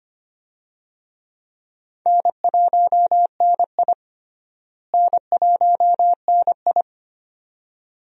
Morse code, heard as a keyed tone, about 25 wpm, 700 Hz. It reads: N1NI N1NI
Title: USA Call Signs - 4 Characters - Repeated 2x - 25wpm